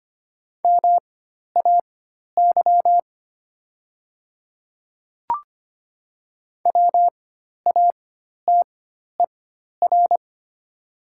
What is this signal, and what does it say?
Morse code, heard as a keyed tone, about 25 wpm, spaced as for 12 wpm, 700 Hz. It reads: MAY WATER
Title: 4x Character Spacing - Top 100 Words - 25wpm